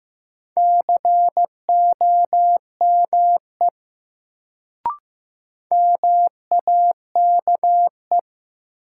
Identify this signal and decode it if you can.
Morse code, heard as a keyed tone, about 15 wpm, 700 Hz. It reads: COME MAKE
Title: Sets of 2 Words - New Word Drill - 15wpm